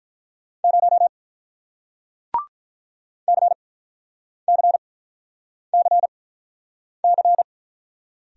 Morse code, heard as a keyed tone, about 48 wpm, spaced as for 9 wpm, 700 Hz. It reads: 0 CCCC